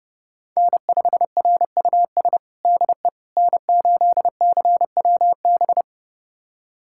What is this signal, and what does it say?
Morse code, heard as a keyed tone, about 32 wpm, 700 Hz. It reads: N5RUS DE N8CWB
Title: USA Callsign DE Callsign - 5 Characters - 30wpm